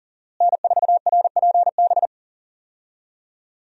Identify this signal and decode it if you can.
Morse code, heard as a keyed tone, about 38 wpm, 700 Hz. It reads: N4RPB
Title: USA Call Signs - 5 Characters - Repeated 3x - 40wpm